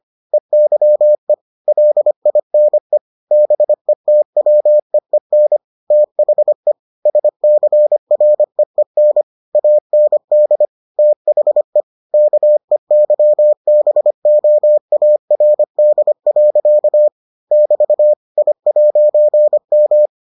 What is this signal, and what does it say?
Morse code, heard as a keyed tone, about 25 wpm, 600 Hz. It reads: EYE LINE BETWEEN THE SCREEN AND THE KEYBOARD. <BT> I'M